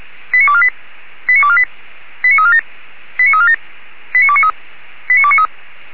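Five-tone selective call (ZVEI), heard as quick runs of five tones, tones of about 70 ms. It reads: 89248 89248 89347 89347 89293 89293
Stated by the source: ZVEI five-tone selective call 85385 received by SDR.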